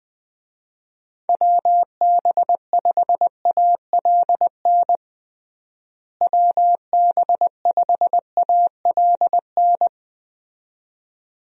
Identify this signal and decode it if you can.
Morse code, heard as a keyed tone, about 21 wpm, 700 Hz. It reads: WB5ALN WB5ALN
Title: USA Call Signs - 6 Characters - Repeated 2x - 20wpm